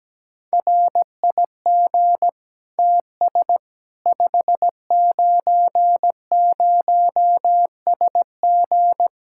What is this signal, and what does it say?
Morse code, heard as a keyed tone, about 17 wpm, 700 Hz. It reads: RIG TS 590SG
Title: K1USN SST QSO Practice 17wpm